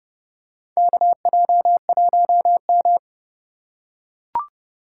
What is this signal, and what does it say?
Morse code, heard as a keyed tone, about 32 wpm, 700 Hz. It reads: KJ1M
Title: USA Call Signs - 4 Characters - 4x Character Spacing - 30wpm